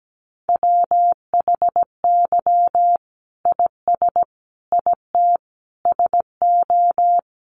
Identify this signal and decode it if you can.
Morse code, heard as a keyed tone, about 17 wpm, 700 Hz. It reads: WHY IS IT SO
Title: Sentences from Top 1500 Words 17wpm